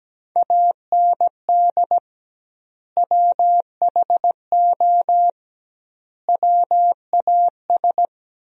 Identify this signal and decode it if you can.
Morse code, heard as a keyed tone, about 17 wpm, 700 Hz. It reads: AND WHO WAS
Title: 2x Word-Spacing - Sentences from Top 100 Words - 17wpm